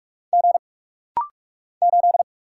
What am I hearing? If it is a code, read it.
Morse code, heard as a keyed tone, about 44 wpm, 700 Hz. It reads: G 8
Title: Single Letter-Number - Rapid-Fire 45wpm